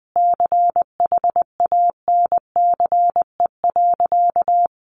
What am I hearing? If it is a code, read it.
Morse code, heard as a keyed tone, about 20 wpm, 700 Hz. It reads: CHANCE.
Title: Sentences from Top 1100 Words 20wpm